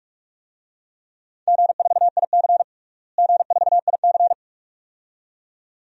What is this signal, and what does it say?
Morse code, heard as a keyed tone, about 42 wpm, 700 Hz. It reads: G4IC G4IC
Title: DXCC-100 Call Signs - 4 Characters - Repeated 2x - 45wpm